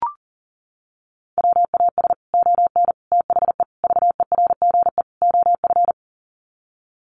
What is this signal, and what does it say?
Morse code, heard as a keyed tone, about 40 wpm, 700 Hz. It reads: WAS ON THE VERGE OF
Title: Sets of 5 Words Encore 40wpm